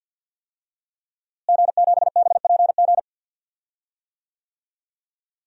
Morse code, heard as a keyed tone, about 51 wpm, 700 Hz. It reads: G7BPG
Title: DXCC-100 Call Signs - 5 Characters - Repeated 2x - 50wpm